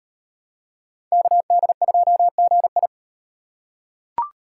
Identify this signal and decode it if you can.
Morse code, heard as a keyed tone, about 35 wpm, 700 Hz. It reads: KD2GI